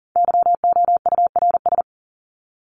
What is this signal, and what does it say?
Morse code, heard as a keyed tone, about 40 wpm, 700 Hz. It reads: YOURS
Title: Top 1751-2000 Words 40wpm